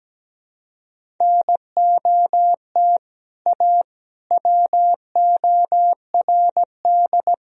Morse code, heard as a keyed tone, about 17 wpm, 700 Hz. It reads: NOT A WORD
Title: Sentences from Top 1000 Words - Review Part 1 - 17wpm